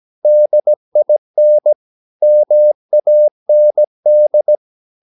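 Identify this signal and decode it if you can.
Morse code, heard as a keyed tone, about 17 wpm, 600 Hz. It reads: DIN MAND
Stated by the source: Slow (QRS) Morse practice at 17 wpm, from CW med Gnister.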